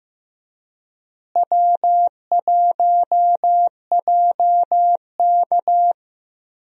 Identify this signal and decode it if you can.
Morse code, heard as a keyed tone, about 15 wpm, 700 Hz. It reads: W1JK
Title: USA Call Signs - 4 Characters - Repeated 3x - 15wpm